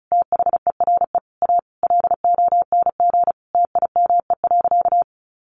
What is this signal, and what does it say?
Morse code, heard as a keyed tone, about 34 wpm, 700 Hz. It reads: THERE A LONG TIME.
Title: Sentences from Top 400 Words 35wpm